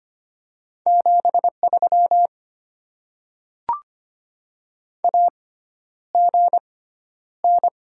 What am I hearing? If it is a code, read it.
Morse code, heard as a keyed tone, about 25 wpm, spaced as for 11 wpm, 700 Hz. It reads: 73 AGN
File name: Top 20 QSO Elements - 6x Character Spacing - 25wpm